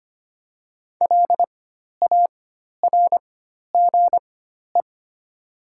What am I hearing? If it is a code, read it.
Morse code, heard as a keyed tone, about 25 wpm, spaced as for 12 wpm, 700 Hz. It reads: LARGE
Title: 4x Character Spacing - Top 200 Words - 25wpm